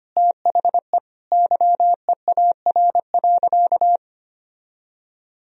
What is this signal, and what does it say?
Morse code, heard as a keyed tone, about 25 wpm, 700 Hz. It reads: THE YEAR.
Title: Sentences from Top 1000 Words - Review Part 1 - 25wpm